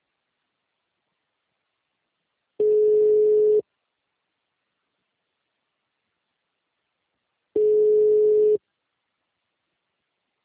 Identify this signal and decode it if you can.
Ringback tone (Europe, 425 Hz).